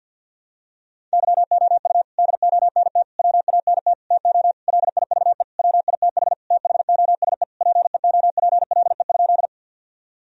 Morse code, heard as a keyed tone, about 51 wpm, 700 Hz. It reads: YOU DONT WANT TO LIVE WITH THOSE PEOPLE?
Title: Sentences from Top 1000 Words - Review Part 1 - 50wpm